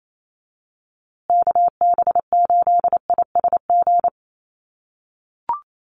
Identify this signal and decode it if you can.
Morse code, heard as a keyed tone, about 28 wpm, 700 Hz. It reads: KB8ISG
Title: USA Call Signs Repeated 2x 28wpm